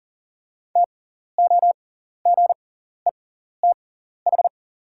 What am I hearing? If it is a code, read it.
Morse code, heard as a keyed tone, about 40 wpm, spaced as for 14 wpm, 690 Hz. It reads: TOGETH
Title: 6x Character Spacing - Top 500 Words - 40wpm